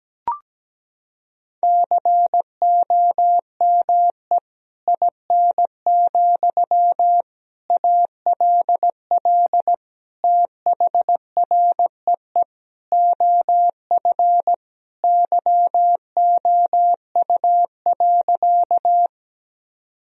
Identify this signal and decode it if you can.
Morse code, heard as a keyed tone, about 17 wpm, 700 Hz. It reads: COME IN, ALL THREE OF YOU.
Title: Sentences from Top 1000 Words - Review Part 1 - 17wpm